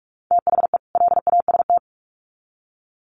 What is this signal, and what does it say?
Morse code, heard as a keyed tone, about 44 wpm, 700 Hz. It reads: THE LAST
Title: Sets of 2 Words 45wpm